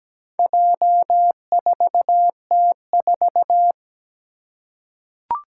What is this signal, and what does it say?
Morse code, heard as a keyed tone, about 17 wpm, 700 Hz. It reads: J4T4
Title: Four Challenging Letter-Number Combinations 17wpm